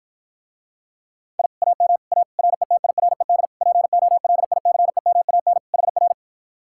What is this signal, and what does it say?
Morse code, heard as a keyed tone, about 52 wpm, 700 Hz. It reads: I AM A RETIRED POLICEMAN HR